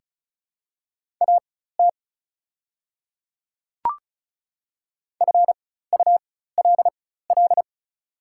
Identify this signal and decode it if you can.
Morse code, heard as a keyed tone, about 36 wpm, spaced as for 16 wpm, 690 Hz. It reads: AT FULL
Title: 4x Character Spacing - Top 500 Words - 35wpm